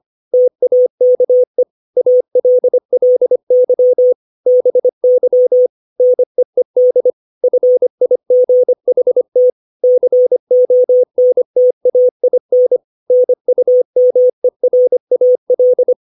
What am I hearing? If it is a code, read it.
Morse code, heard as a keyed tone, about 25 wpm, 500 Hz. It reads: TAKE ALLY BY NEED FIGHT CONTAIN NUMERAL
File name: 632 words / 632 Words 25WPM